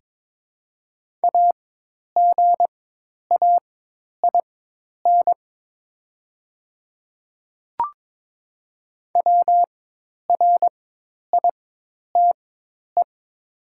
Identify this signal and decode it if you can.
Morse code, heard as a keyed tone, about 22 wpm, spaced as for 10 wpm, 700 Hz. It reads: AGAIN WRITE
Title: 4x Character Spacing - Top 500 Words - 22wpm